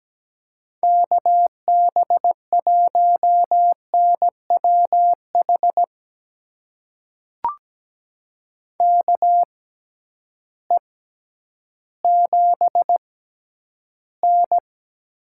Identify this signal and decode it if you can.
Morse code, heard as a keyed tone, about 17 wpm, spaced as for 10 wpm, 700 Hz. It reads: KB1NWH KE7N
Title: USA Call Signs - 6 Characters - 6x Character Spacing - 17wpm